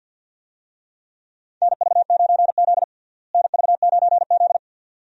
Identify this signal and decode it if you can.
Morse code, heard as a keyed tone, about 48 wpm, 700 Hz. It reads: NV9Z NV9Z
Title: USA Call Signs - 4 Characters - Repeated 2x - 50wpm